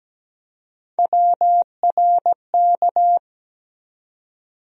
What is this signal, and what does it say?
Morse code, heard as a keyed tone, about 17 wpm, 700 Hz. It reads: WRK
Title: USA Call Sign Postfixes - 3 Letters 17wpm